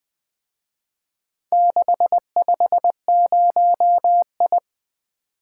Morse code, heard as a keyed tone, about 24 wpm, 700 Hz. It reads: 650I
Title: Speed-Racing - Four Letter-Number Combinations - 20wpm